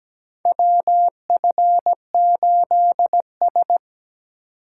Morse code, heard as a keyed tone, about 17 wpm, 700 Hz. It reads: WF8S
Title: USA Call Signs - 4 Characters - 8x Character Spacing - 17wpm